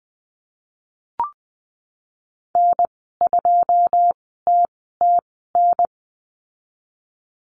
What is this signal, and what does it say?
Morse code, heard as a keyed tone, about 20 wpm, spaced as for 14 wpm, 700 Hz. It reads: N2TTN
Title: 122 Lesson 20g - Number 3 - Call Signs 20wpm